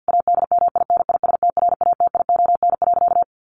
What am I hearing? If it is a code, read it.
Morse code, heard as a keyed tone, about 50 wpm, 700 Hz. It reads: ADMINISTRATION.
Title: Sentences from Top 1300 Words 50wpm